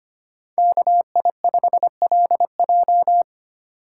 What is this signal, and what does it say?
Morse code, heard as a keyed tone, about 33 wpm, 700 Hz. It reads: KI5LJ